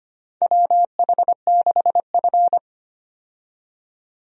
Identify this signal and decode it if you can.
Morse code, heard as a keyed tone, about 25 wpm, 700 Hz. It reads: WH6F
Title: USA Call Signs - 4 Characters - 25wpm